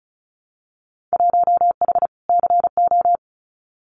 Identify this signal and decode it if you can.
Morse code, heard as a keyed tone, about 36 wpm, 700 Hz. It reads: 1H CO
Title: Winter Field Day Exchanges 35wpm